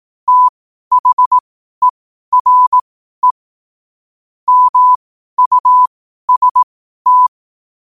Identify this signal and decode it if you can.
Morse code, heard as a keyed tone, about 18 wpm, spaced as for 12 wpm, 1000 Hz. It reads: THERE MUST
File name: qotdfiles / 12wpm